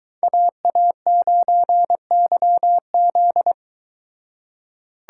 Morse code, heard as a keyed tone, about 22 wpm, 700 Hz. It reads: AA9YZ